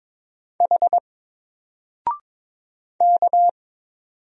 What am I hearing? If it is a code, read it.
Morse code, heard as a keyed tone, about 22 wpm, 700 Hz. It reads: H K
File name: Single-Letter-Number-22wpm